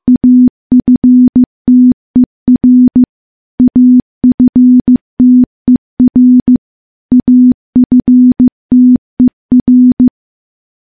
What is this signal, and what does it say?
Morse code, heard as a keyed tone, about 15 wpm, 250 Hz. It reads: AFTER AFTER AFTER
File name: morse-training-15WPM